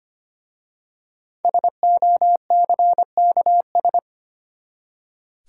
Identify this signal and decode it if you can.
Morse code, heard as a keyed tone, about 25 wpm, 700 Hz. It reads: SOCKS
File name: practice / 25wpm / Easy Words with Hints - 25wpm